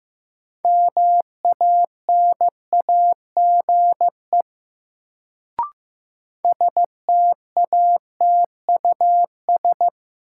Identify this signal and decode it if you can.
Morse code, heard as a keyed tone, about 15 wpm, 700 Hz. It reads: MANAGE STATUS
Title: Sentences from Top 1400 Words - New Word Drill - 15wpm